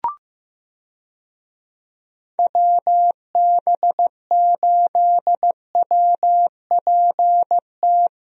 Morse code, heard as a keyed tone, about 16 wpm, 700 Hz. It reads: WB8WPT
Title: USA Call Signs - 6 Characters - Repeated 3x - 15wpm